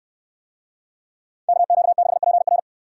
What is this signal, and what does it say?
Morse code, heard as a keyed tone, about 69 wpm, 700 Hz. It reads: KQ6PR